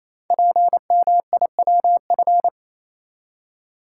Morse code, heard as a keyed tone, about 29 wpm, 700 Hz. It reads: PMIWF
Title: Code Groups - 5 Letter:Numbers - 28wpm